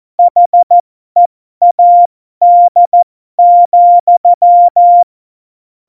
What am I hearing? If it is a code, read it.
Morse code, heard as a keyed tone, about 14 wpm, spaced as for 12 wpm, 700 Hz. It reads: HEAD,
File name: Morsekurs / Witze